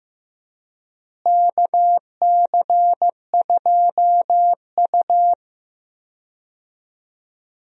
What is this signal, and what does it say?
Morse code, heard as a keyed tone, about 15 wpm, 700 Hz. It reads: KC2U
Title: USA Call Signs - 4 Characters - Repeated 2x - 15wpm